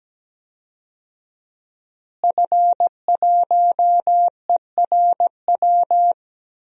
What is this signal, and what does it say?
Morse code, heard as a keyed tone, about 17 wpm, 700 Hz. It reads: F1ERW
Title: DXCC-100 Call Signs - 5 Characters - Repeated 2x - 17wpm